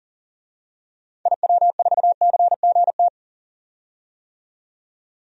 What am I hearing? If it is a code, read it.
Morse code, heard as a keyed tone, about 40 wpm, 700 Hz. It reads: IW4CGT
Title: DXCC-100 Call Signs - 6 Characters - 40wpm